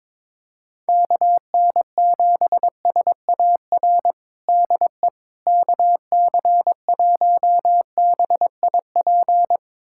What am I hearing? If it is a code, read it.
Morse code, heard as a keyed tone, about 21 wpm, 700 Hz. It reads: KN7SAR DE KC1BIP
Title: USA Callsign DE Callsign 22wpm